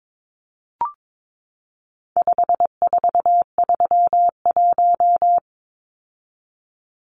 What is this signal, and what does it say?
Morse code, heard as a keyed tone, about 22 wpm, 700 Hz. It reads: 5431
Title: Numbers - 4-Digit 22wpm